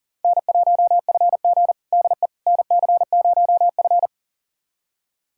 Morse code, heard as a keyed tone, about 40 wpm, 700 Hz. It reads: N1FG DE NC0F
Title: USA Callsign DE Callsign - 4 Characters - 40wpm